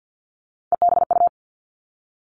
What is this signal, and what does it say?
Morse code, heard as a keyed tone, about 50 wpm, 700 Hz. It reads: EBU
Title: Three Challenging Letter-Number Combinations 50wpm